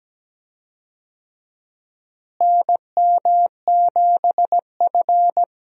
Morse code, heard as a keyed tone, about 17 wpm, 700 Hz. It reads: NM7F
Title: USA Call Signs - 4 Characters - Repeated 3x - 17wpm